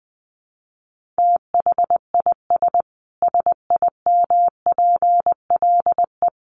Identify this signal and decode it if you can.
Morse code, heard as a keyed tone, about 20 wpm, 700 Hz. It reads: THIS SIMPLE
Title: Sentences from Top 1500 Words 20wpm